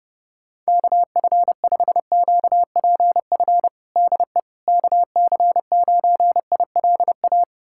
Morse code, heard as a keyed tone, about 32 wpm, 700 Hz. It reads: KF5QPF DE KC9ILA
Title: USA Callsign DE Callsign 30wpm